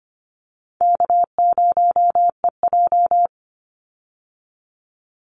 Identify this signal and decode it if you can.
Morse code, heard as a keyed tone, about 25 wpm, 700 Hz. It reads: K0EJ
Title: Easy Call Signs - 25wpm